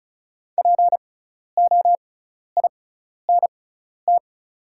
Morse code, heard as a keyed tone, about 36 wpm, spaced as for 12 wpm, 700 Hz. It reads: POINT
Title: 6x Character Spacing - Top 300 Words - 35wpm